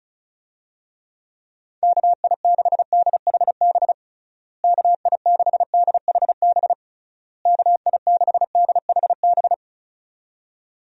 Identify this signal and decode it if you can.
Morse code, heard as a keyed tone, about 34 wpm, 700 Hz. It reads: KI6DHB KI6DHB KI6DHB
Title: USA Call Signs - 6 Characters - Repeated 3x - 35wpm